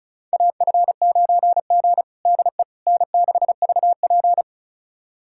Morse code, heard as a keyed tone, about 34 wpm, 700 Hz. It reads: AF9G DE N6VP
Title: USA Callsign DE Callsign - 4 Characters - 35wpm